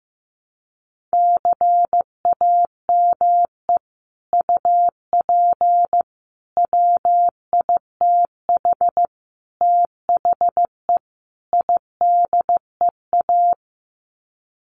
Morse code, heard as a keyed tone, about 15 wpm, 700 Hz. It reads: CAME UP WITH THE IDEA
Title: Sets of 5 Words Encore2 15wpm